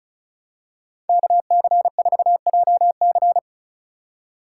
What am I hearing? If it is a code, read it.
Morse code, heard as a keyed tone, about 34 wpm, 700 Hz. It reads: KC4JC
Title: USA Call Signs - 5 Characters - 4x Character Spacing - 35wpm